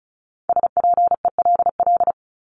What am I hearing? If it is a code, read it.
Morse code, heard as a keyed tone, about 34 wpm, 700 Hz. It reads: SPELL
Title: Top 500 Words Repeated 35wpm